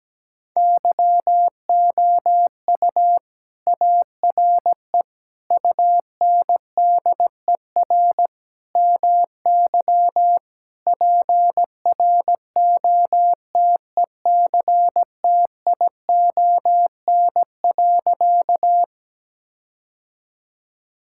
Morse code, heard as a keyed tone, about 17 wpm, 700 Hz. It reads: YOU ARE UNDER MY PROTECTION.
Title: Sentences from Top 1600 Words 17wpm